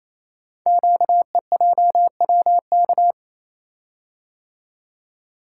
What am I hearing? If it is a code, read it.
Morse code, heard as a keyed tone, about 27 wpm, 700 Hz. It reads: QEJWK
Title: Code Groups - 5 Letters - 28wpm